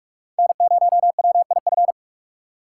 Morse code, heard as a keyed tone, about 56 wpm, 700 Hz. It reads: N0WIF